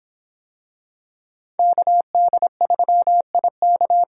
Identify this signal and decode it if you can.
Morse code, heard as a keyed tone, about 26 wpm, 700 Hz. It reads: KD3IK